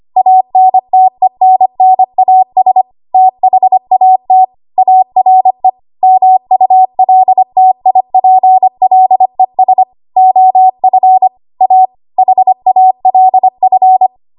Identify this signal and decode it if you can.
Morse code, heard as a keyed tone, about 25 wpm, 750 Hz. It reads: ANTENNAS THAT ARE MULTIPLES OF A HALF
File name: Archive / 25 WPM / 25WPM